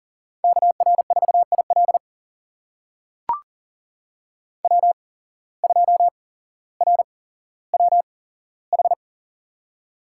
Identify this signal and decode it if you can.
Morse code, heard as a keyed tone, about 40 wpm, spaced as for 17 wpm, 700 Hz. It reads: KR4IL W2RWH
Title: USA Call Signs - 5 Characters - 8x Character Spacing - 40wpm